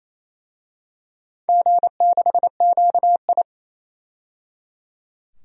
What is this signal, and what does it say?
Morse code, heard as a keyed tone, about 28 wpm, 700 Hz. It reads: G6QI
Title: DXCC-100 Call Signs - 4 Characters - 28wpm